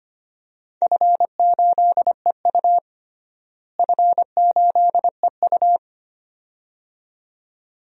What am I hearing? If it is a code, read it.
Morse code, heard as a keyed tone, about 25 wpm, 700 Hz. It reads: F8EU F8EU
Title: DXCC-100 Call Signs - 4 Characters - Repeated 2x - 25wpm